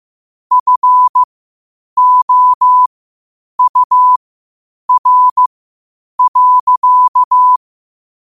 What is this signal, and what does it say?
Morse code, heard as a keyed tone, about 15 wpm, spaced as for 8 wpm, 1000 Hz. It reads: FOUR.